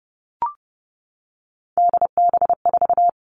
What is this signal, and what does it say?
Morse code, heard as a keyed tone, about 30 wpm, 700 Hz. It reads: DB4
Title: Three Challenging Letter-Number Combinations 30wpm